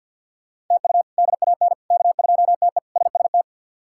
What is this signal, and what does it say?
Morse code, heard as a keyed tone, about 50 wpm, 700 Hz. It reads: TU DAN K2TE SST
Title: K1USN SST QSO Practice - 50wpm